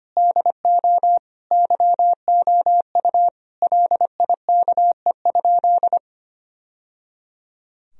Morse code, heard as a keyed tone, about 25 wpm, 700 Hz. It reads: DO YOU LIKE?
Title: Sentences from Top 1000 Words - Review Part 3 - 25wpm